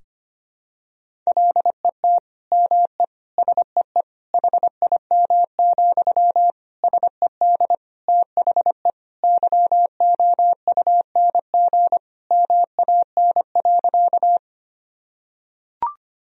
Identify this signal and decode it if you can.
Morse code, heard as a keyed tone, about 25 wpm, 700 Hz. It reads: LET ME SEE HIM, SED THE YOUNG MAN.
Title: Sentences from Top 400 Words and Abbreviations 25wpm